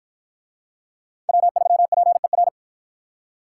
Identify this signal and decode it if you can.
Morse code, heard as a keyed tone, about 48 wpm, 700 Hz. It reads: W3PER